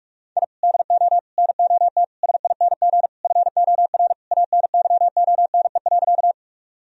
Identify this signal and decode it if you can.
Morse code, heard as a keyed tone, about 46 wpm, 700 Hz. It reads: I DO NOT SING FOR ANYONE.
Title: Sentences from Top 1000 Words - Review Part 2 - 45wpm